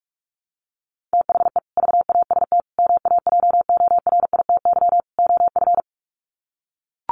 Morse code, heard as a keyed tone, about 44 wpm, 700 Hz. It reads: THE VAST MAJORITY OF
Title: Sets of 4 Words 45wpm